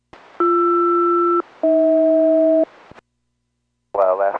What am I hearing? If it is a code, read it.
Aircraft SELCAL, heard as two pairs of tones, with speech. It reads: BR-AH